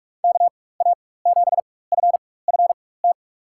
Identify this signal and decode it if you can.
Morse code, heard as a keyed tone, about 44 wpm, spaced as for 21 wpm, 700 Hz. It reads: KA7FFT
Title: USA Call Signs - 6 Characters - 4x Character Spacing - 45wpm